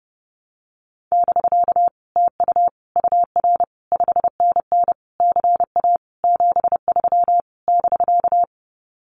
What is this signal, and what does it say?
Morse code, heard as a keyed tone, about 30 wpm, 700 Hz. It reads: <BK> TU UR 5NN CA 73 <BK>
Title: Common QSO Phrases Encore 30wpm